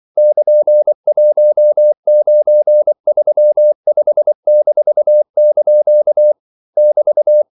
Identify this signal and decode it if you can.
Morse code, heard as a keyed tone, about 24 wpm, 600 Hz. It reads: <KN>1935-) <BT>
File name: CW / Practice / EnglishQuotations0424